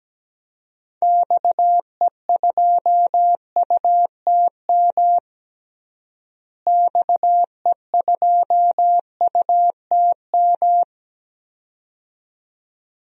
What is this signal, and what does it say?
Morse code, heard as a keyed tone, about 17 wpm, 700 Hz. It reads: XE2UTM XE2UTM
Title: DXCC-100 Call Signs - 6 Characters - Repeated 2x - 17wpm